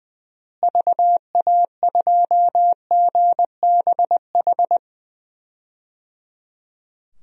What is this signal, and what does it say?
Morse code, heard as a keyed tone, about 20 wpm, 700 Hz. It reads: VA2GBH
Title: DXCC-100 Call Signs - 6 Characters - Repeated 3x - 20wpm